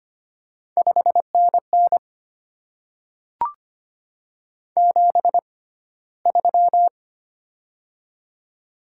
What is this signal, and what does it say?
Morse code, heard as a keyed tone, about 25 wpm, spaced as for 15 wpm, 700 Hz. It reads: 5NN 73
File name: Top 20 QSO Elements - 6x Character Spacing - 25wpm